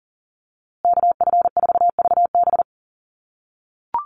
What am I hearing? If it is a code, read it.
Morse code, heard as a keyed tone, about 40 wpm, 700 Hz. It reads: KF4VB
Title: USA Call Signs Repeated 2x 40wpm